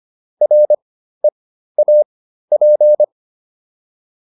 Morse code, heard as a keyed tone, about 25 wpm, spaced as for 13 wpm, 600 Hz. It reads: REAP